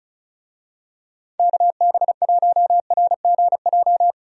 Morse code, heard as a keyed tone, about 34 wpm, 700 Hz. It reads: KB1RGJ
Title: USA Call Signs - 6 Characters - 35wpm